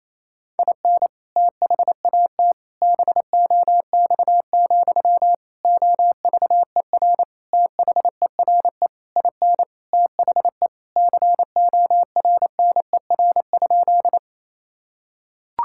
Morse code, heard as a keyed tone, about 28 wpm, 700 Hz. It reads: IN THAT BOX, OVER THERE IN THE CORNER?
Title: Sentences from Top 1600 Words 28wpm